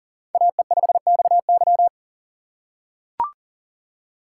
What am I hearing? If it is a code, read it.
Morse code, heard as a keyed tone, about 40 wpm, 700 Hz. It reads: AE5XY
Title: USA Call Signs - 5 Characters - 2x Character Spacing - 40wpm